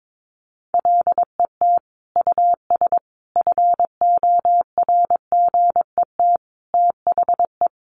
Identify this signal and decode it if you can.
Morse code, heard as a keyed tone, about 22 wpm, 700 Hz. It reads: LET US FORGET THE
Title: Sentences from Top 1200 Words 22wpm